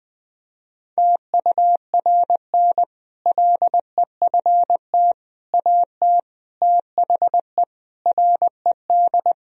Morse code, heard as a keyed tone, about 20 wpm, 700 Hz. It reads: TURN LEFT AT THE RED
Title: Sentences from Top 1000 Words - Review Part 2 - 20wpm